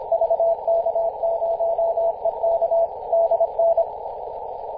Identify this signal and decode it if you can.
Morse code, heard as a keyed tone, about 28 wpm, 670 Hz, with noise. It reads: VK0EK DN
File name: posts / 2016-mar-29-0657z-vk0ek-24-904mhz-cw